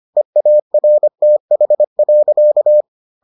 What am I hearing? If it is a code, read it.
Morse code, heard as a keyed tone, about 25 wpm, 600 Hz. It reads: EARTH.